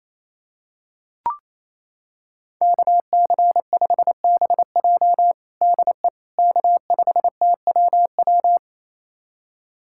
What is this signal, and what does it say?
Morse code, heard as a keyed tone, about 28 wpm, 700 Hz. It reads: KC5BJ DE K5TWW
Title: USA Callsign DE Callsign - 5 Characters - 28wpm